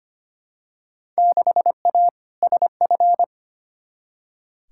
Morse code, heard as a keyed tone, about 25 wpm, 700 Hz. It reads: 6A SF
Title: Field Day Exchanges - 25wpm